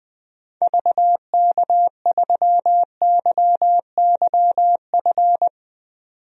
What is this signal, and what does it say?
Morse code, heard as a keyed tone, about 20 wpm, 700 Hz. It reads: VK3YYF
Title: DXCC-100 Call Signs - 6 Characters - 20wpm